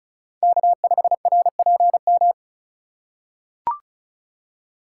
Morse code, heard as a keyed tone, about 36 wpm, 700 Hz. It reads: K5RPM
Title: USA Call Signs - 5 Characters - 6x Character Spacing - 35wpm